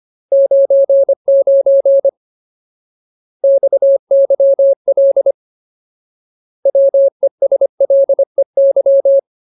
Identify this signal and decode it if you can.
Morse code, heard as a keyed tone, about 25 wpm, 550 Hz. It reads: 99 XYL WESLEY
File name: Ultimate Word List / 25WPM Ultimate